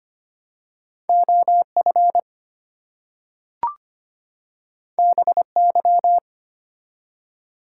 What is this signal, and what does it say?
Morse code, heard as a keyed tone, about 25 wpm, 700 Hz. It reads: OF BY
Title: Two Letter Words - 25wpm